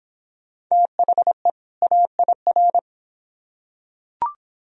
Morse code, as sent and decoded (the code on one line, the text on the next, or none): - .... / .- .. .-.
THE AIR